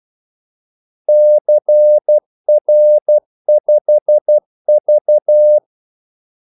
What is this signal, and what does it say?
Morse code, heard as a keyed tone, about 12 wpm, 600 Hz. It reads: CR5V
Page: 2 ekstra ordmellemrum mellem kaldesignalerne